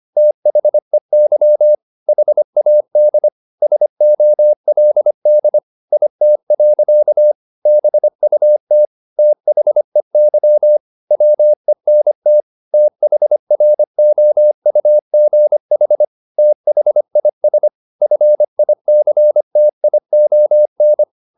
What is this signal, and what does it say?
Morse code, heard as a keyed tone, about 25 wpm, 600 Hz. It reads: THEY HAD SOLD IT. BUT THEY WENT THROUGH THIS FICTION
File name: the old man and the sea part 1 25wpm